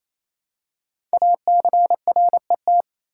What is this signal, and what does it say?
Morse code, heard as a keyed tone, about 28 wpm, 700 Hz. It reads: ACRET
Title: Code Groups - 5 Letters - 28wpm